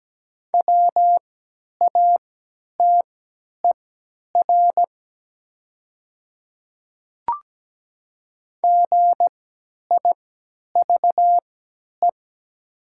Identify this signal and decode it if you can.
Morse code, heard as a keyed tone, about 17 wpm, spaced as for 10 wpm, 700 Hz. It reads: WATER GIVE
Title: 3x Character Spacing - Top 200 Words - 17wpm